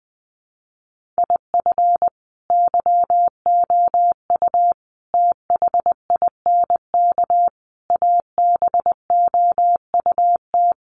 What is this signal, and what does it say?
Morse code, heard as a keyed tone, about 19 wpm, 700 Hz. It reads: IF YOU THINK ABOUT
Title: Sets of 5 Words Encore2 20wpm